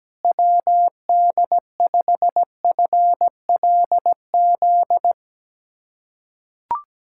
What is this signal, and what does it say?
Morse code, heard as a keyed tone, about 17 wpm, 700 Hz. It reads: WD5FLZ
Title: USA Callsign DE Callsign - 6 Characters - 17wpm